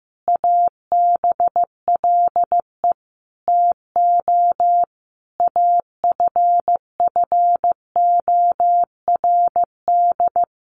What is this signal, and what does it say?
Morse code, heard as a keyed tone, about 15 wpm, 700 Hz. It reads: ABLE TO AFFORD
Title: Sentences from Top 1100 Words 15wpm